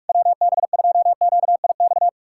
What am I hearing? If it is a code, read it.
Morse code, heard as a keyed tone, about 69 wpm, spaced as for 56 wpm, 700 Hz. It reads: WB2QIX
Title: Speed-Racing Common QSO Phrases - 45wpm